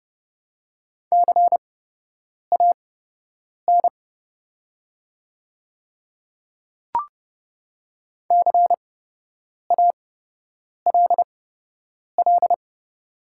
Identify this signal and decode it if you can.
Morse code, heard as a keyed tone, about 30 wpm, spaced as for 8 wpm, 700 Hz. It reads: CAN CALL
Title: 8x Character Spacing - Top 300 Words - 30wpm